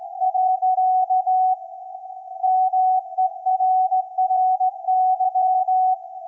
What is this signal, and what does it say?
Morse code, heard as a keyed tone, about 17 wpm, 750 Hz. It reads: . MERRY
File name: iz7slz / VLF / SAQ 24-12-2024